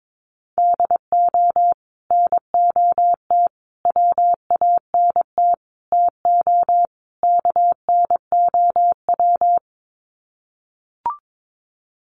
Morse code, heard as a keyed tone, about 22 wpm, 700 Hz. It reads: DO NOT WANT TO KNOW
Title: Sets of 5 Words Encore 22wpm